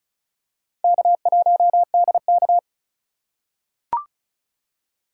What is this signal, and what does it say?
Morse code, heard as a keyed tone, about 36 wpm, 700 Hz. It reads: K1DK
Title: USA Call Signs - 4 Characters - 4x Character Spacing - 35wpm